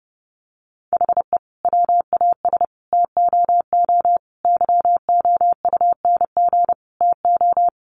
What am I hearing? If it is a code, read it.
Morse code, heard as a keyed tone, about 30 wpm, 700 Hz. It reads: HE WAS TOO YOUNG TO
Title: Sentences from Top 600 Words 30wpm